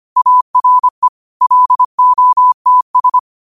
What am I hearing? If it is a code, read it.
Morse code, heard as a keyed tone, about 25 wpm, 1000 Hz. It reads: ARE LOTS
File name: qotdfiles / 25wpm